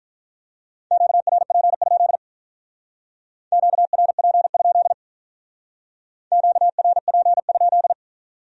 Morse code, heard as a keyed tone, about 45 wpm, 700 Hz. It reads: QRP? QRP? QRP?